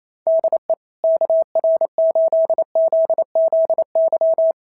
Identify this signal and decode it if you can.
Morse code, heard as a keyed tone, about 28 wpm, 650 Hz. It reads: DE KR8ZZY
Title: SOTA QSO Practice 28wpm